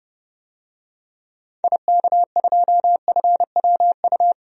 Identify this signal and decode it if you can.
Morse code, heard as a keyed tone, about 30 wpm, 700 Hz. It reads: IK2FWU
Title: DXCC-100 Call Signs - 6 Characters - 30wpm